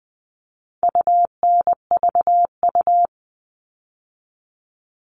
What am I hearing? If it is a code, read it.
Morse code, heard as a keyed tone, about 20 wpm, 700 Hz. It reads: UNVU
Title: All Cut Numbers - 4-Digit 20wpm